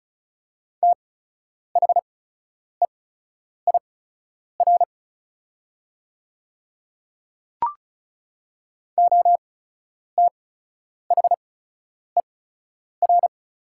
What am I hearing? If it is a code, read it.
Morse code, heard as a keyed tone, about 36 wpm, spaced as for 10 wpm, 700 Hz. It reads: THEIR OTHER
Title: 8x Character Spacing - Top 100 Words - 35wpm